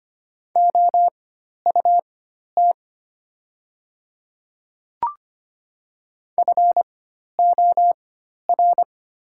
Morse code, heard as a keyed tone, about 25 wpm, spaced as for 12 wpm, 700 Hz. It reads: OUT FOR
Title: 4x Character Spacing - Top 300 Words - 25wpm